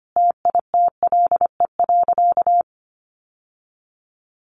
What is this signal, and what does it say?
Morse code, heard as a keyed tone, about 25 wpm, 700 Hz. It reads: TITLE.
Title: Sentences from Top 1200 Words 25wpm